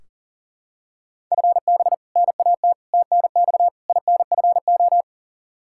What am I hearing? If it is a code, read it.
Morse code, heard as a keyed tone, about 40 wpm, 700 Hz. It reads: FB NAT TNX INFO
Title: Practice QSO - Exchange 3 - More Info - 40wpm